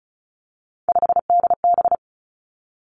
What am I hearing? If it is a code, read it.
Morse code, heard as a keyed tone, about 36 wpm, 700 Hz. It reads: 5DB
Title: Three Challenging Letter-Number Combinations 35wpm